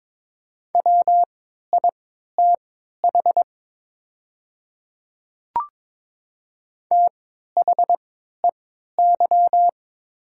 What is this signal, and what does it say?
Morse code, heard as a keyed tone, about 22 wpm, spaced as for 12 wpm, 700 Hz. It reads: WITH THEY
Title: 3x Character Spacing - Top 200 Words - 22wpm